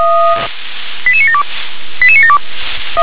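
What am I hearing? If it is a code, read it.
Five-tone selective call (ZVEI), heard as quick runs of five tones, tones of about 80 ms. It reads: 88972 88972
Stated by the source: ZVEI five-tone call received by SDR.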